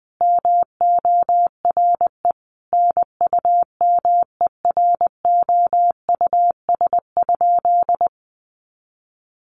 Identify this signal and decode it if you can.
Morse code, heard as a keyed tone, about 20 wpm, 700 Hz. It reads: MORE NUMEROUS?
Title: Sentences from Top 1400 Words 20wpm